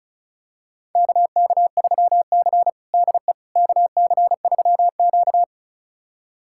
Morse code, heard as a keyed tone, about 36 wpm, 700 Hz. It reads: KK3C DE KC3Q
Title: USA Callsign DE Callsign - 4 Characters - 35wpm